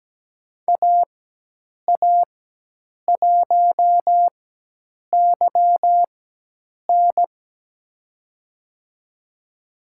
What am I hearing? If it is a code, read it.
Morse code, heard as a keyed tone, about 17 wpm, spaced as for 8 wpm, 700 Hz. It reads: AA1YN
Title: USA Call Signs - 5 Characters - 4x Character Spacing - 17wpm